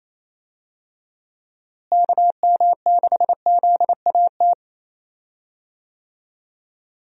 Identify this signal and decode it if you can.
Morse code, heard as a keyed tone, about 28 wpm, 700 Hz. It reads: KM6ZAT
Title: USA Call Signs - 6 Characters - Repeated 2x - 28wpm